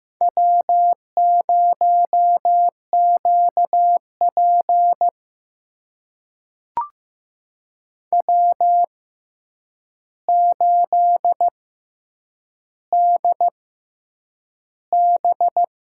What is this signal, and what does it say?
Morse code, heard as a keyed tone, about 15 wpm, spaced as for 8 wpm, 700 Hz. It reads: W0QP W8DB
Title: USA Call Signs - 4 Characters - 6x Character Spacing - 15wpm